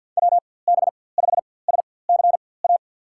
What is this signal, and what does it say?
Morse code, heard as a keyed tone, about 52 wpm, spaced as for 23 wpm, 700 Hz. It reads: WB5SCA
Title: USA Call Signs - 4x Character Spacing - 50wpm